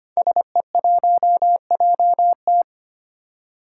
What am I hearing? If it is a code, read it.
Morse code, heard as a keyed tone, about 25 wpm, 700 Hz. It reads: SE1JT
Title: Five Challenging Letter-Number Combinations 25wpm